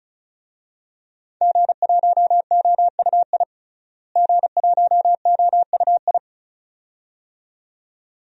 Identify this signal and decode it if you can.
Morse code, heard as a keyed tone, about 36 wpm, 700 Hz. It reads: G1OUI G1OUI
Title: DXCC-100 Call Signs - 5 Characters - Repeated 2x - 35wpm